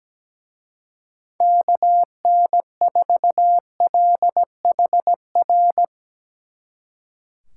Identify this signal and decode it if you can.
Morse code, heard as a keyed tone, about 19 wpm, 700 Hz. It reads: KN4LHR
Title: Speed Racing - USA Call Signs 17wpm